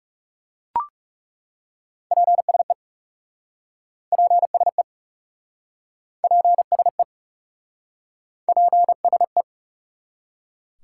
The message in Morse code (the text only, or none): PSE PSE PSE PSE